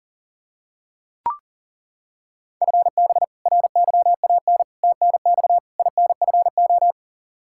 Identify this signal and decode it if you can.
Morse code, heard as a keyed tone, about 40 wpm, 700 Hz. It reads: FB RYAN TNX INFO
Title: Practice QSO - Exchange 3 - More Info - 40wpm